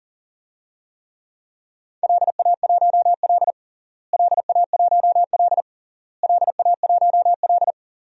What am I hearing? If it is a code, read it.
Morse code, heard as a keyed tone, about 37 wpm, 700 Hz. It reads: LA1L LA1L LA1L
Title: DXCC-100 Call Signs - 4 Characters - Repeated 3x - 40wpm